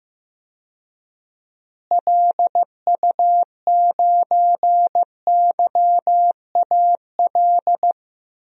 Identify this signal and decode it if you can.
Morse code, heard as a keyed tone, about 15 wpm, 700 Hz. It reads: LU9YAL
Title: DXCC-100 Call Signs - 6 Characters - Repeated 3x - 15wpm